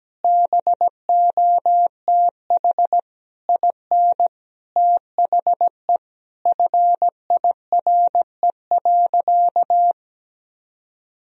Morse code, heard as a keyed tone, about 17 wpm, 700 Hz. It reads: BOTH IN THE FIRE.
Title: Sentences from Top 1000 Words - Review Part 1 - 17wpm